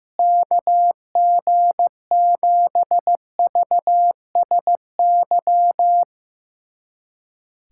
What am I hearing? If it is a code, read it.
Morse code, heard as a keyed tone, about 15 wpm, 700 Hz. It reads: KG7VSY
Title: USA Callsign DE Callsign 15wpm